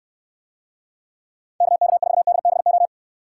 Morse code, heard as a keyed tone, about 69 wpm, 700 Hz. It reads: KK4DBO